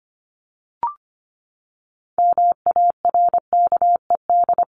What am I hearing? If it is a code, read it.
Morse code, heard as a keyed tone, about 25 wpm, 700 Hz. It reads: MARKED
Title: ED Suffix - Top 100 Words with Base Words 25wpm